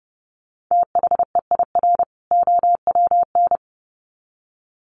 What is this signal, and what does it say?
Morse code, heard as a keyed tone, about 30 wpm, 700 Hz. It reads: THEIR OWN
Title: Sets of 2 Words 30wpm